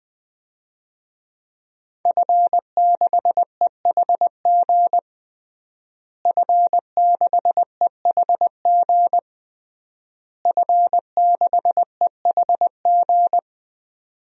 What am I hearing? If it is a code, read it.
Morse code, heard as a keyed tone, about 20 wpm, 700 Hz. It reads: F6EHG F6EHG F6EHG
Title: DXCC-100 Call Signs - 5 Characters - Repeated 3x - 20wpm